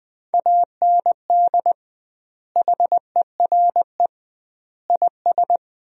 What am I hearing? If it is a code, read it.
Morse code, heard as a keyed tone, about 20 wpm, 700 Hz. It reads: AND HERE IS
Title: 2x Word-Spacing - Sentences from Top 500 Words - 20wpm